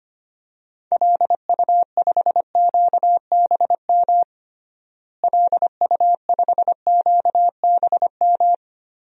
Morse code, heard as a keyed tone, about 25 wpm, 700 Hz. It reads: LU5QBM LU5QBM
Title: DXCC-100 Call Signs - 6 Characters - Repeated 2x - 25wpm